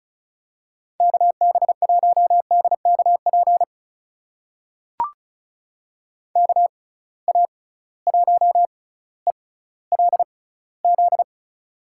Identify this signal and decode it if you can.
Morse code, heard as a keyed tone, about 36 wpm, spaced as for 18 wpm, 700 Hz. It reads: KB1DKP KA1ELZ
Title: USA Call Signs - 6 Characters - 6x Character Spacing - 35wpm